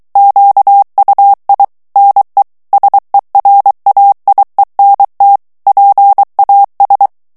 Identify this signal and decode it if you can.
Morse code, heard as a keyed tone, about 23 wpm, 800 Hz. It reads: QUI NE SERAIENT PAS
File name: FRANCAIS TEXTE CLAIR 2 VITESSE 22 MOTS MINUTES